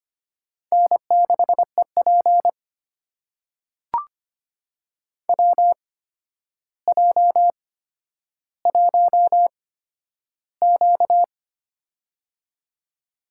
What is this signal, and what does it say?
Morse code, heard as a keyed tone, about 25 wpm, spaced as for 11 wpm, 700 Hz. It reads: N6EP WJ1Q
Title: USA Call Signs - 8x Character Spacing - 25wpm